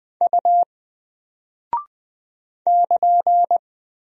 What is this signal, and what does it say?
Morse code, heard as a keyed tone, about 20 wpm, 700 Hz. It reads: U <KN>
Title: Single Character (Letter:Number:Punctuation:Prosign) Alt 20wpm